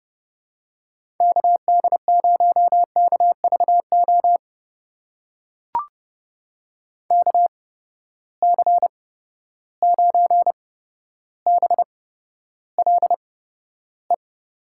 Morse code, heard as a keyed tone, about 30 wpm, spaced as for 13 wpm, 700 Hz. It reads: KD0KVO KC9BLE
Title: USA Call Signs - 8x Character Spacing - 30wpm